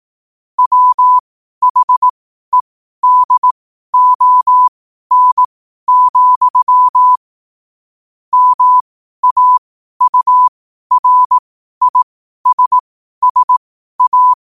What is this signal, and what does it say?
Morse code, heard as a keyed tone, about 18 wpm, spaced as for 12 wpm, 1000 Hz. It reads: WHEDON, MAURISSA
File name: qotdfiles / 12wpm